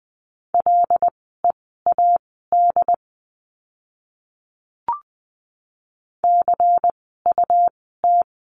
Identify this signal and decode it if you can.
Morse code, heard as a keyed tone, about 20 wpm, spaced as for 15 wpm, 700 Hz. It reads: LEAD CUT
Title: 101 Lesson 17f - Letter u - Words 20wpm